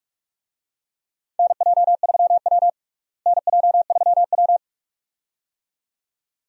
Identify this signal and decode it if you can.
Morse code, heard as a keyed tone, about 46 wpm, 700 Hz. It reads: NJ3W NJ3W
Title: USA Call Signs - 4 Characters - Repeated 2x - 45wpm